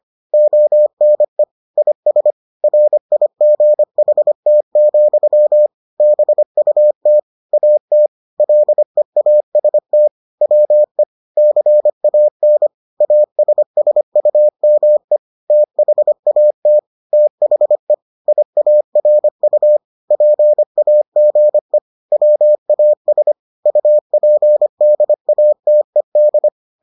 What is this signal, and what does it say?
Morse code, heard as a keyed tone, about 25 wpm, 600 Hz. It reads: ONE IS RIGHT, BUT AT LEAST WE CAN ASSUME THAT THE IARU PAGE WAS UPDATED